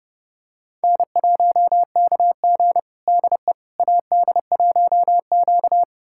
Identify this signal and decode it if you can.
Morse code, heard as a keyed tone, about 30 wpm, 700 Hz. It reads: N1KG DE AD1Q
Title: USA Callsign DE Callsign - 4 Characters - 30wpm